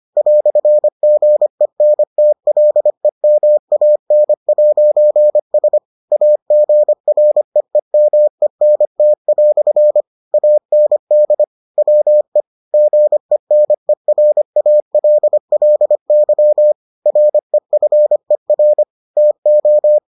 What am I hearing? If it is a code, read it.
Morse code, heard as a keyed tone, about 25 wpm, 600 Hz. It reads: "GENTLEMAN'S AGREEMENT" AND WE GENERALLY REFER TO